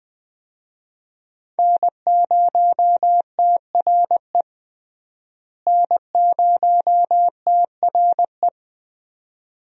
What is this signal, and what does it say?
Morse code, heard as a keyed tone, about 20 wpm, 700 Hz. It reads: N0TRE N0TRE
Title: USA Call Signs - 5 Characters - Repeated 2x - 20wpm